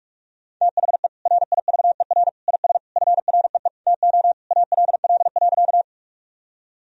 Morse code, heard as a keyed tone, about 46 wpm, 700 Hz. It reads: THE RIVER IS FREE TO ALL.
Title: Sentences from Top 1000 Words - Review Part 1 - 45wpm